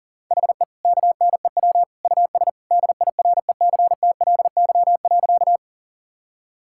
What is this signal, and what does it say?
Morse code, heard as a keyed tone, about 40 wpm, 700 Hz. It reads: HE KNEW US DIRECTLY.
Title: Sentences from Top 1000 Words - Review Part 5 - 40wpm